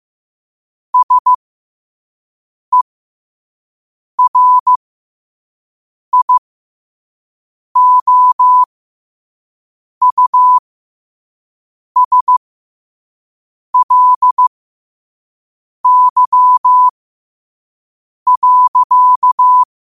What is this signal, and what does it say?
Morse code, heard as a keyed tone, about 15 wpm, spaced as for 5 wpm, 1000 Hz. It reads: SERIOUSLY.